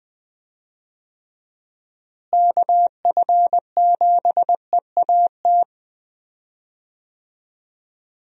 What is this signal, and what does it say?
Morse code, heard as a keyed tone, about 20 wpm, 700 Hz. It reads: KF7EAT
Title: USA Call Signs - 6 Characters - Repeated 3x - 20wpm